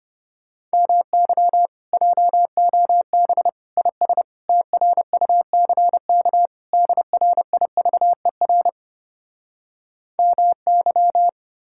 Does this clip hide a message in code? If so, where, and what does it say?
Yes: Morse code from 0.7 to 11.3 s, heard as a keyed tone, about 33 wpm, 700 Hz. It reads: MY JOB IS TRUCK DRIVER MY